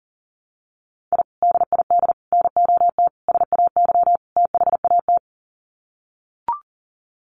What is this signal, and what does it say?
Morse code, heard as a keyed tone, about 37 wpm, 700 Hz. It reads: I DID NOT SAY THAT
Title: Sets of 5 Words Encore2 40wpm